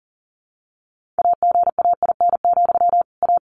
Morse code, heard as a keyed tone, about 40 wpm, 700 Hz. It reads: AGAIN, A
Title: Sentences from Top 1400 Words 40wpm